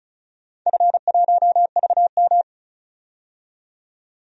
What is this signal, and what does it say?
Morse code, heard as a keyed tone, about 36 wpm, 700 Hz. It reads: F1VM
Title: DXCC-100 Call Signs - 4 Characters - 35wpm